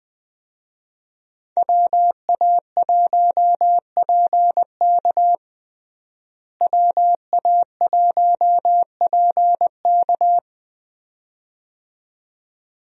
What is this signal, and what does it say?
Morse code, heard as a keyed tone, about 20 wpm, 700 Hz. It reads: WA1PK WA1PK
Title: USA Call Signs - 5 Characters - Repeated 2x - 20wpm